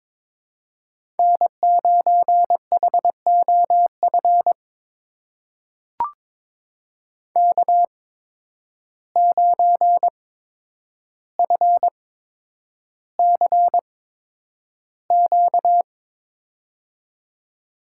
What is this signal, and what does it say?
Morse code, heard as a keyed tone, about 22 wpm, spaced as for 9 wpm, 700 Hz. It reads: N9HOF K9FCQ
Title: USA Call Signs - 5 Characters - 8x Character Spacing - 22wpm